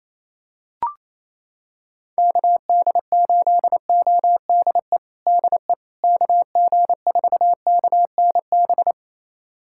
Morse code, heard as a keyed tone, about 28 wpm, 700 Hz. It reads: KD8ODE DE KG4KNB
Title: USA Callsign DE Callsign - 6 Characters - 28wpm